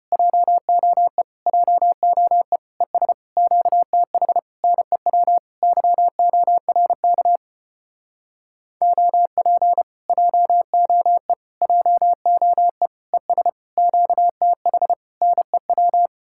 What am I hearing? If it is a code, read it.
Morse code, heard as a keyed tone, about 33 wpm, 700 Hz. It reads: JOE JOE ES QTH NEW YORK OP JOE JOE ES QTH NEW